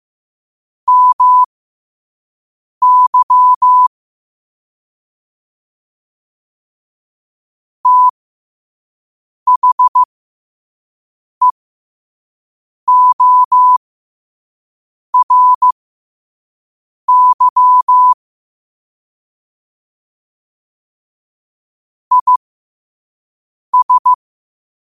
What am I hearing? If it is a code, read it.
Morse code, heard as a keyed tone, about 15 wpm, spaced as for 5 wpm, 1000 Hz. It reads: MY THEORY IS
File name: qotdfiles / 5wpm